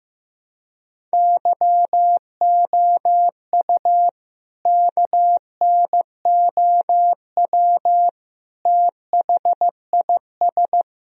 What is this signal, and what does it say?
Morse code, heard as a keyed tone, about 15 wpm, 700 Hz. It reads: YOU KNOW THIS
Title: Sentences from Top 1000 Words - Review Part 2 - 15wpm